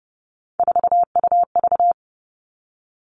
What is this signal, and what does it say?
Morse code, heard as a keyed tone, about 30 wpm, 700 Hz. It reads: 4UV
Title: All Cut Numbers - 3-Digit 30wpm